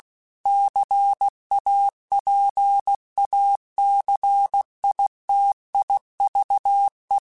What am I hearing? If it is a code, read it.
Morse code, heard as a keyed tone, about 16 wpm, 800 Hz. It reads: CAPACITIVE
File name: Technical Subjects / Technical Subjects-mp3s / XC @15